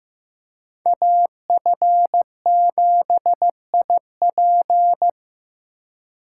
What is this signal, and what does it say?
Morse code, heard as a keyed tone, about 17 wpm, 700 Hz. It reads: AF7IP
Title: Speed Racing - USA 2x2 Call Signs 15wpm